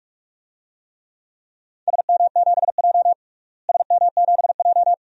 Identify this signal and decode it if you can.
Morse code, heard as a keyed tone, about 44 wpm, 700 Hz. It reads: SM7J SM7J
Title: DXCC-100 Call Signs - 4 Characters - Repeated 2x - 45wpm